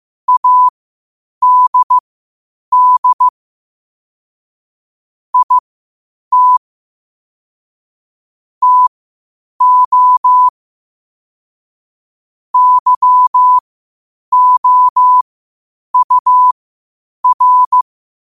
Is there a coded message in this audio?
Yes: Morse code from 0.3 to 17.8 s, heard as a keyed tone, about 15 wpm, spaced as for 8 wpm, 1000 Hz. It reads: ADD IT TO YOUR